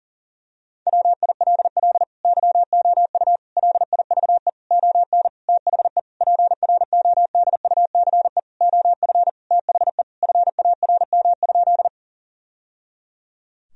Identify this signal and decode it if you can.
Morse code, heard as a keyed tone, about 40 wpm, 700 Hz. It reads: WILL YOU LIVE ON THE PRODUCE OF THE FARM?
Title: Sentences from Top 1000 Words - Review Part 3 - 40wpm